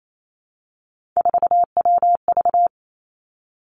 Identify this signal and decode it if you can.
Morse code, heard as a keyed tone, about 28 wpm, 700 Hz. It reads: 4WV
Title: Three Challenging Letter-Number Combinations 28wpm